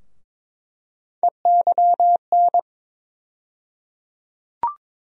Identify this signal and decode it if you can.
Morse code, heard as a keyed tone, about 22 wpm, 700 Hz. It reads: EYN
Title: USA Call Sign Postfixes - 3 Letters 22wpm